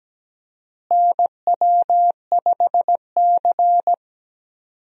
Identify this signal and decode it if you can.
Morse code, heard as a keyed tone, about 17 wpm, 700 Hz. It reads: NW5C
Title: USA Call Signs - 4 Characters - Repeated 3x - 17wpm